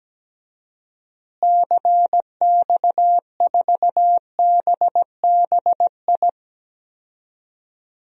Morse code, heard as a keyed tone, about 17 wpm, 700 Hz. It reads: CX4BBI
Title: DXCC-100 Call Signs - 6 Characters - Repeated 2x - 17wpm